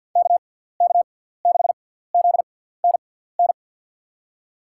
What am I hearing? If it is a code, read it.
Morse code, heard as a keyed tone, about 52 wpm, spaced as for 17 wpm, 700 Hz. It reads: KK6ZNN
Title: USA Call Signs - 6 Characters - 6x Character Spacing - 50wpm